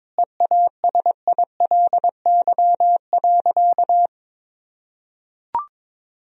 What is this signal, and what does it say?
Morse code, heard as a keyed tone, about 22 wpm, 700 Hz. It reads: EASILY.
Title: Sentences from Top 1000 Words - Review Part 1 - 22wpm